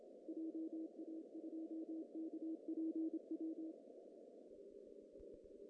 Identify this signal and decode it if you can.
Morse code, heard as a keyed tone, about 27 wpm, 320 Hz. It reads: JA2KPW
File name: CW / cqww-cw-k5tr